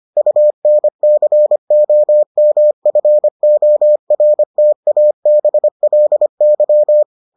Morse code, heard as a keyed tone, about 25 wpm, 600 Hz. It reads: UNCOMFORTABLY